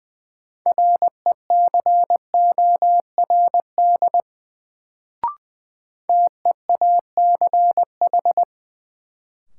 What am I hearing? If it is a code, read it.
Morse code, heard as a keyed tone, about 21 wpm, 700 Hz. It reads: RECORD TEACH
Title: Sentences from Top 500 Words - New Word Drill - 20wpm